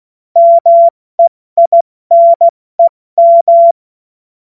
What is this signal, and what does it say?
Morse code, heard as a keyed tone, about 16 wpm, spaced as for 14 wpm, 680 Hz. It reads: MEINEM
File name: Morsekurs / Witze / Witz 26. April 2022 70 F80